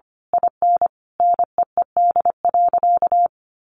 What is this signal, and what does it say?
Morse code, heard as a keyed tone, about 25 wpm, 700 Hz. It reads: IN NEED.
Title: Sentences from Top 1200 Words 25wpm